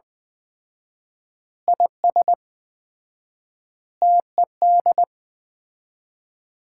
Morse code, heard as a keyed tone, about 20 wpm, 700 Hz. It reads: IS TED
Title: Common QSO Phrases - 4x Word-Spacing - 20wpm